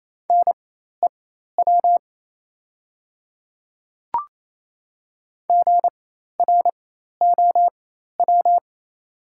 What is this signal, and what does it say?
Morse code, heard as a keyed tone, about 27 wpm, spaced as for 13 wpm, 690 Hz. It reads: NEW GROW
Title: 4x Character Spacing - Top 300 Words - 28wpm